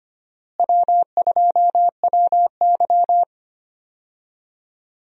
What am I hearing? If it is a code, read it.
Morse code, heard as a keyed tone, about 25 wpm, 700 Hz. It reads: W2WY
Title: USA Call Signs - 4 Characters - Repeated 3x - 25wpm